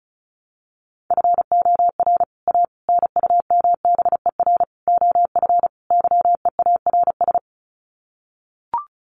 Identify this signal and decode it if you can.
Morse code, heard as a keyed tone, about 34 wpm, 700 Hz. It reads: FOR A NUMBER OF YEARS
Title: Sets of 5 Words Encore 35wpm